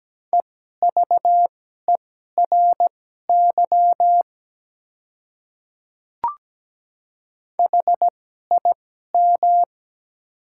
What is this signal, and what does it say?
Morse code, heard as a keyed tone, about 17 wpm, spaced as for 12 wpm, 700 Hz. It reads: EVERY HIM
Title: 2x Character Spacing - Top 200 Words - 17wpm